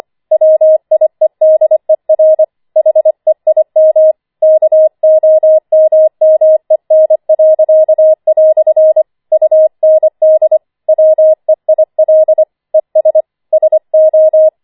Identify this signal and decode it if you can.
Morse code, heard as a keyed tone, about 25 wpm, 630 Hz. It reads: WIEDER HEIM KOMMEN." UND WEIL ES SO
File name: 25wpm